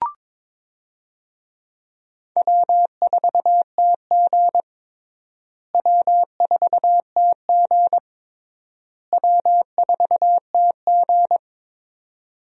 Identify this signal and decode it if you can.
Morse code, heard as a keyed tone, about 22 wpm, 700 Hz. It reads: W4TG W4TG W4TG
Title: USA Call Signs - 4 Characters - Repeated 3x - 22wpm